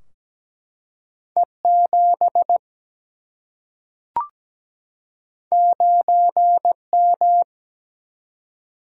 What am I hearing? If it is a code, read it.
Morse code, heard as a keyed tone, about 17 wpm, 700 Hz. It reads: E 7 9 M
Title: Single-Letter Single-Number 17wpm